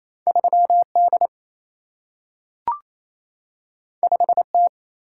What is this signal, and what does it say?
Morse code, heard as a keyed tone, about 28 wpm, 700 Hz. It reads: 3D 5T
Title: Single-Letter Single-Number 28wpm